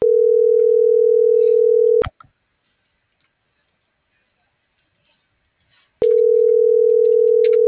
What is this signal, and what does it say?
Ringback tone (North America), 440 + 480 Hz.